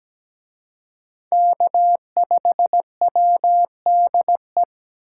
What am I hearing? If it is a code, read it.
Morse code, heard as a keyed tone, about 20 wpm, 700 Hz. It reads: K5WDE
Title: Speed Racing - USA 1x3 Call Signs 17wpm